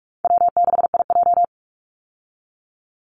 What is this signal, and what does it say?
Morse code, heard as a keyed tone, about 44 wpm, 700 Hz. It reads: W6IJ
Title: Easy Call Signs Repeated 2x - 45wpm